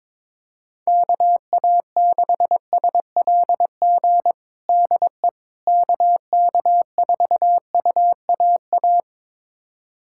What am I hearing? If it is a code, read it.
Morse code, heard as a keyed tone, about 22 wpm, 700 Hz. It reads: KA6SLG DE KK4UAA
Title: USA Callsign DE Callsign - 6 Characters - 22wpm